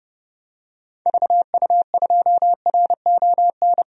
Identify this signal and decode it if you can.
Morse code, heard as a keyed tone, about 30 wpm, 700 Hz. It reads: VU2RON
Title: DXCC-100 Call Signs - 6 Characters - 30wpm